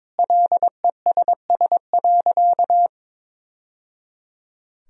Morse code, heard as a keyed tone, about 22 wpm, 700 Hz. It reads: LESS.
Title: Sentences from Top 1000 Words - Review Part 3 - 22wpm